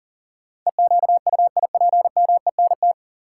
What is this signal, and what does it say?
Morse code, heard as a keyed tone, about 40 wpm, 700 Hz. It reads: EQUIPMENT
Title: Sentences from Top 700 Words - New Word Drill - 40wpm